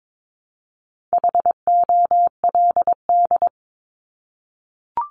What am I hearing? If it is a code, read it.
Morse code, heard as a keyed tone, about 22 wpm, 700 Hz. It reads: HOLD
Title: Top 500 Words 22wpm